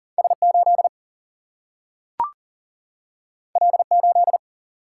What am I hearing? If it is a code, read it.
Morse code, heard as a keyed tone, about 40 wpm, 700 Hz. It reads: S 8 L 8
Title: Single-Letter Single-Number 40wpm